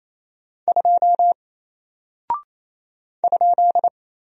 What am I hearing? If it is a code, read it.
Morse code, heard as a keyed tone, about 28 wpm, 700 Hz. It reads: 2 ?
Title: Single Character (Letter:Number:Punctuation:Prosign) Alt 28wpm